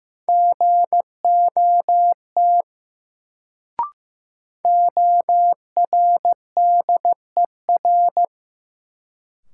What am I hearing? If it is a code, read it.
Morse code, heard as a keyed tone, about 15 wpm, 700 Hz. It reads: GOT ORDER
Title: Sets of 2 Words - New Word Drill - 15wpm